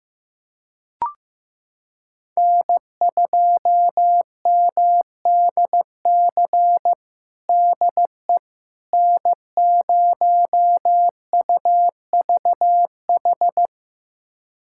Morse code, heard as a keyed tone, about 15 wpm, 700 Hz. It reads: N2MDC DE N0UVH
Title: USA Callsign DE Callsign - 5 Characters - 15wpm